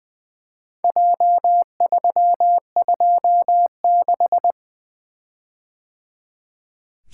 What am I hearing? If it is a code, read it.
Morse code, heard as a keyed tone, about 20 wpm, 700 Hz. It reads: J326
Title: Four Letter-Number Combinations - 20wpm